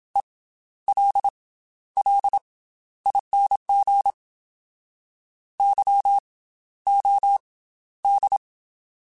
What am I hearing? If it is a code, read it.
Morse code, heard as a keyed tone, about 27 wpm, spaced as for 13 wpm, 800 Hz. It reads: ELLING YOD
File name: Special INGs / Special INGs-mp3s / Slow ing @25